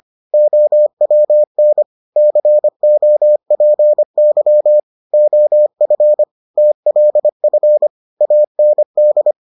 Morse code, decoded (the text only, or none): OWN COPY OF TLF AND